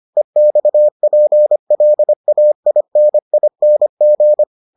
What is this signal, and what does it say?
Morse code, heard as a keyed tone, about 25 wpm, 600 Hz. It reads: EXPLAINING